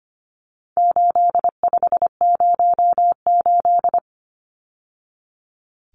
Morse code, heard as a keyed tone, about 25 wpm, 700 Hz. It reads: 8508
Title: Numbers - 4-Digit 25wpm